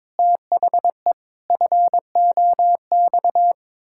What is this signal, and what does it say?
Morse code, heard as a keyed tone, about 22 wpm, 700 Hz. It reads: THE FOX
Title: Pangram Challenge v2 - 22wpm